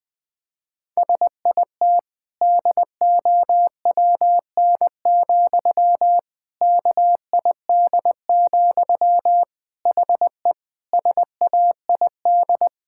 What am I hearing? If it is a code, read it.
Morse code, heard as a keyed tone, about 20 wpm, 700 Hz. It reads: SIT DOWN, KID, HE SAID
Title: Sentences from Top 1600 Words 20wpm